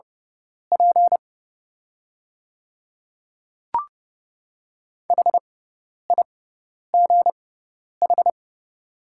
Morse code, heard as a keyed tone, about 30 wpm, spaced as for 10 wpm, 700 Hz. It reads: P HIGH
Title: 6x Character Spacing - Top 200 Words - 30wpm